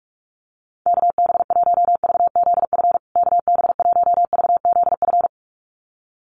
Morse code, heard as a keyed tone, about 44 wpm, 700 Hz. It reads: KB1VZF KB1VZF
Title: USA Call Signs Repeated 2x 45wpm